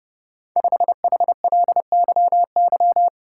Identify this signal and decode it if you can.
Morse code, heard as a keyed tone, about 30 wpm, 700 Hz. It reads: 5HLYY
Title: Code Groups - 5 Letter:Numbers - 30wpm